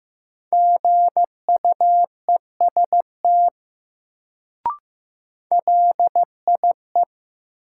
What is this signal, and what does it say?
Morse code, heard as a keyed tone, about 15 wpm, 700 Hz. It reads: GUEST LIE
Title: Sentences from Top 1600 Words - New Word Drill - 15wpm